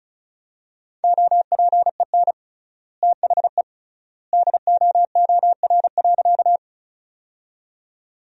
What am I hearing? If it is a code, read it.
Morse code, heard as a keyed tone, about 36 wpm, 700 Hz. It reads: OPEN THE DOOR.
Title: 3x Word-Spacing - Sentences from Top 300 Words - 35wpm